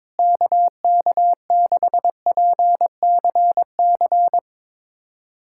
Morse code, heard as a keyed tone, about 22 wpm, 700 Hz. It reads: KK6PCC
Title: USA Callsign DE Callsign 22wpm